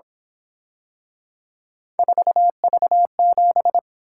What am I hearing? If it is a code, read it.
Morse code, heard as a keyed tone, about 26 wpm, 700 Hz. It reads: 4V7
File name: Speed-Racing - Three Letter-Number Combinations - 22wpm